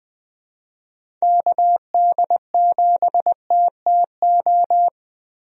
Morse code, heard as a keyed tone, about 20 wpm, 700 Hz. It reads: KD7TTO
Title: USA Call Signs - 6 Characters - Repeated 3x - 20wpm